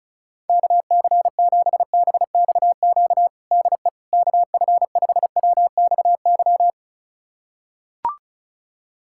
Morse code, heard as a keyed tone, about 36 wpm, 700 Hz. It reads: KC7BXQ DE KF5WXY
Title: USA Callsign DE Callsign 35wpm